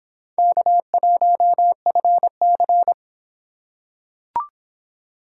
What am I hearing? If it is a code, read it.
Morse code, heard as a keyed tone, about 24 wpm, 700 Hz. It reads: K1FC